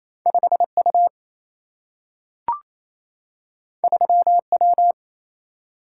Morse code, heard as a keyed tone, about 28 wpm, 700 Hz. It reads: 5 U 3 W
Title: Single-Letter Single-Number 28wpm